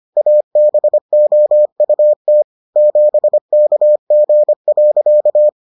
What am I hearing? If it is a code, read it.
Morse code, heard as a keyed tone, about 25 wpm, 600 Hz. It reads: ABOUT 7KG.